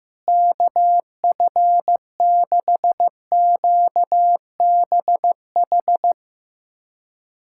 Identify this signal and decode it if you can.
Morse code, heard as a keyed tone, about 16 wpm, 700 Hz. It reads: KF6QBH
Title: USA Callsign DE Callsign 15wpm